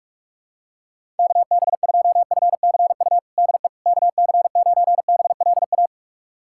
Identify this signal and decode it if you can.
Morse code, heard as a keyed tone, about 45 wpm, 700 Hz. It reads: KB2FCU DE KC9BLA
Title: USA Callsign DE Callsign 45wpm